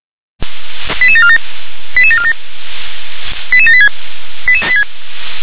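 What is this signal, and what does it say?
Five-tone selective call (ZVEI), heard as quick runs of five tones, tones of about 70 ms. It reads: 88747 88747 88786
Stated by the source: ZVEI five-tone call 88646 received on an SDR.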